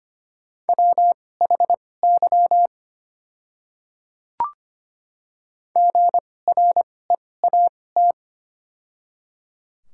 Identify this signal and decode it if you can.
Morse code, heard as a keyed tone, about 25 wpm, spaced as for 18 wpm, 700 Hz. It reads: WHY GREAT
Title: 2x Character Spacing - Top 200 Words - 25wpm